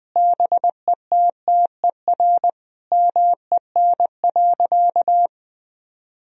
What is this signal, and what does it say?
Morse code, heard as a keyed tone, about 20 wpm, 700 Hz. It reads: BETTER MEN.
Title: Sentences from Top 1000 Words - Review Part 3 - 20wpm